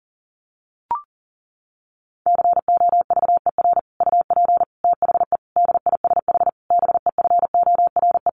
Morse code, heard as a keyed tone, about 40 wpm, 700 Hz. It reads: COVER UP THE DISH BEFORE
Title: Sentences from Top 1300 Words 40wpm